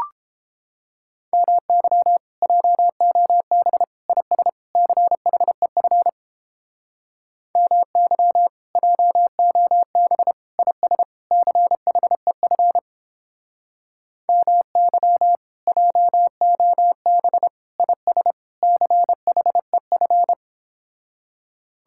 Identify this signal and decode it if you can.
Morse code, heard as a keyed tone, about 29 wpm, 700 Hz. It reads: MY JOB IS CHEF MY JOB IS CHEF MY JOB IS CHEF